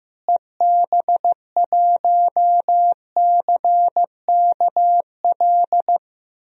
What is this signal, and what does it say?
Morse code, heard as a keyed tone, about 16 wpm, 700 Hz. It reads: EB1CKL
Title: DXCC-100 Call Signs - 6 Characters - Repeated 2x - 15wpm